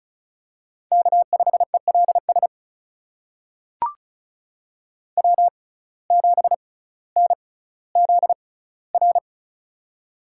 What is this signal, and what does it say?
Morse code, heard as a keyed tone, about 36 wpm, spaced as for 18 wpm, 700 Hz. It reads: K5ELS W7NZR
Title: USA Call Signs - 5 Characters - 6x Character Spacing - 35wpm